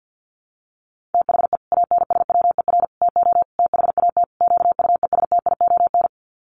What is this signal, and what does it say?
Morse code, heard as a keyed tone, about 50 wpm, 700 Hz. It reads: THE ANSWER TO THAT QUESTION
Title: Sets of 5 Words Encore2 50wpm